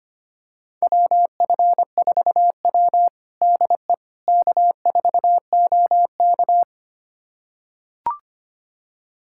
Morse code, heard as a keyed tone, about 25 wpm, 700 Hz. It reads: WF4W DE K4OK
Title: USA Callsign DE Callsign - 4 Characters - 25wpm